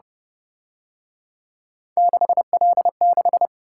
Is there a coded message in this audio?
Yes: Morse code from 2.0 to 3.5 s, heard as a keyed tone, about 28 wpm, 700 Hz. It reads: 6L6